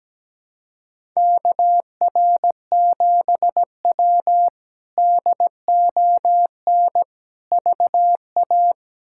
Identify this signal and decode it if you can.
Morse code, heard as a keyed tone, about 17 wpm, 700 Hz. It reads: KR7W DON VA
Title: K1USN SST QSO Practice - 17wpm